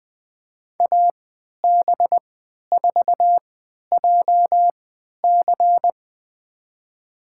Morse code, heard as a keyed tone, about 21 wpm, spaced as for 11 wpm, 700 Hz. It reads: AB4JC
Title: USA Call Signs - 5 Characters - 3x Character Spacing - 20wpm